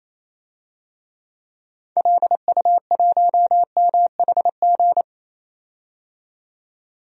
Morse code, heard as a keyed tone, about 28 wpm, 700 Hz. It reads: LU1MHG
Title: DXCC-100 Call Signs - 6 Characters - 28wpm